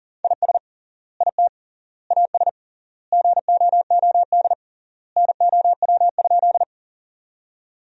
Morse code, heard as a keyed tone, about 41 wpm, 700 Hz. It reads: IS IT AS GOOD NOW?
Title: 3x Word-Spacing - Sentences from Top 200 Words - 40wpm